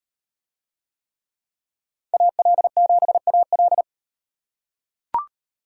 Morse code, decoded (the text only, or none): AL7AL